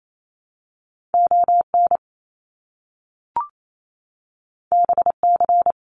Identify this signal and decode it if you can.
Morse code, heard as a keyed tone, about 28 wpm, 700 Hz. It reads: ON BC
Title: Canadian Provinces and Territories - Repeated 3x - 28wpm